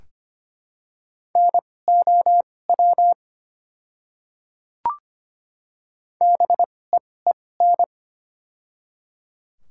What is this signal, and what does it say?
Morse code, heard as a keyed tone, about 25 wpm, spaced as for 18 wpm, 700 Hz. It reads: NOW BEEN
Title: 2x Character Spacing - Top 100 Words - 25wpm